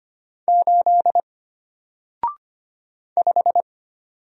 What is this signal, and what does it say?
Morse code, heard as a keyed tone, about 25 wpm, 700 Hz. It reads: 8 5
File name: Single-Letter-Number-25wpm